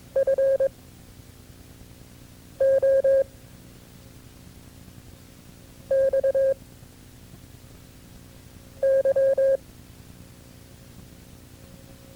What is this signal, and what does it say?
Morse code, heard as a keyed tone, about 22 wpm, 540 Hz. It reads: F O X Y
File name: FreeFiles / CW-practice / KB6MT-tape2-side1